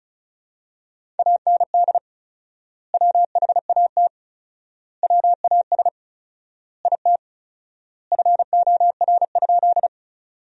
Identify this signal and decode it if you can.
Morse code, heard as a keyed tone, about 36 wpm, 700 Hz. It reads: AND WHAT WAS IT FOR?
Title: 4x Word-Spacing - Sentences from Top 100 Words - 35wpm